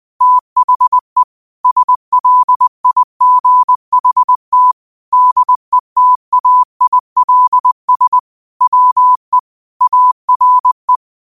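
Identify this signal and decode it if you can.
Morse code, heard as a keyed tone, about 20 wpm, 1000 Hz. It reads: THE SLIGHT DETAILS WE ARE
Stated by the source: Quote of the day sent in Morse at 20 words per minute.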